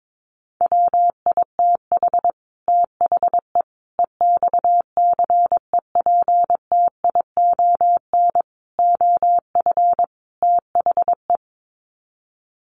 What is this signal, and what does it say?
Morse code, heard as a keyed tone, about 22 wpm, 700 Hz. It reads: WITH THE EXCEPTION OF THE
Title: Sets of 5 Words Encore 22wpm